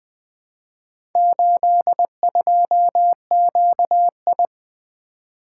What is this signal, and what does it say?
Morse code, heard as a keyed tone, about 20 wpm, 700 Hz. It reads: 82QI
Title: Four Letter-Number Combinations - 20wpm